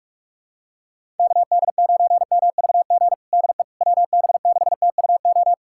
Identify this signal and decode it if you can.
Morse code, heard as a keyed tone, about 46 wpm, 700 Hz. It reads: KD9MVG DE WB6TUO
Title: USA Callsign DE Callsign 45wpm